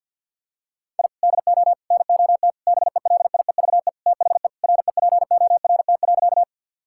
Morse code, heard as a keyed tone, about 52 wpm, 700 Hz. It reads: I DO NOT BELIEVE THE REPORT.
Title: Sentences from Top 1000 Words - Review Part 6 - 50wpm